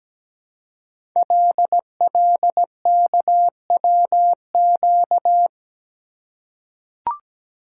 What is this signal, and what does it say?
Morse code, heard as a keyed tone, about 17 wpm, 700 Hz. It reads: LLKWQ
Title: Code Groups - 5 Letter:Numbers - 17wpm